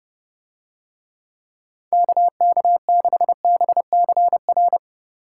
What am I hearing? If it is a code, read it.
Morse code, heard as a keyed tone, about 28 wpm, 700 Hz. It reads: KK6BCR